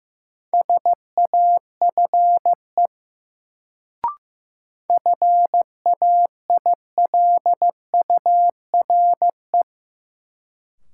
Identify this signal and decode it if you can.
Morse code, heard as a keyed tone, about 15 wpm, 700 Hz. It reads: SAFE FAILURE
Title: Sentences from Top 900 Words - New Word Drill - 15wpm